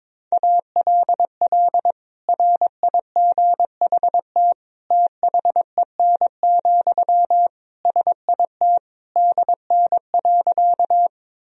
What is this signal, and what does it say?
Morse code, heard as a keyed tone, about 22 wpm, 700 Hz. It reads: ALL RIGHT THEN, SIT DN.
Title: Sentences from Top 500 Words and Abbreviations 22wpm